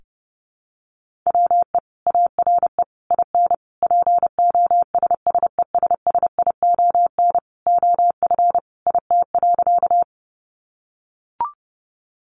Morse code, heard as a keyed tone, about 30 wpm, 700 Hz. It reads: WE ARE IN POSSESSION OF IT.
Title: Sentences from Top 1400 Words 30wpm